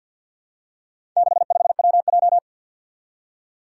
Morse code, heard as a keyed tone, about 53 wpm, 700 Hz. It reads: 65WJ
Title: Four Challenging Letter-Number Combinations 50wpm